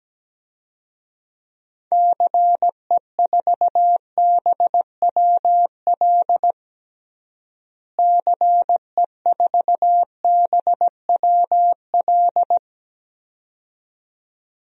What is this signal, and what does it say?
Morse code, heard as a keyed tone, about 17 wpm, 700 Hz. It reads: CE4BWL CE4BWL
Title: DXCC-100 Call Signs - 6 Characters - Repeated 2x - 17wpm